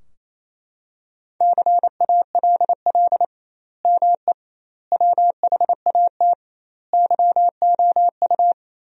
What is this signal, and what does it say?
Morse code, heard as a keyed tone, about 28 wpm, 700 Hz. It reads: CALL ME WHAT YOU
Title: 2x Word-Spacing - Sentences from Top 200 Words - 28wpm